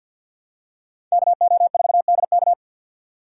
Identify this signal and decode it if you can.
Morse code, heard as a keyed tone, about 46 wpm, 700 Hz. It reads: KO4DK
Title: USA Call Signs - 3x Character Spacing - 50wpm